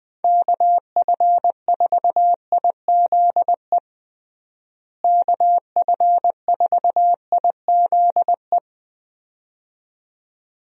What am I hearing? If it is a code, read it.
Morse code, heard as a keyed tone, about 20 wpm, 700 Hz. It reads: KF4IZE KF4IZE
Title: USA Call Signs - 6 Characters - Repeated 2x - 20wpm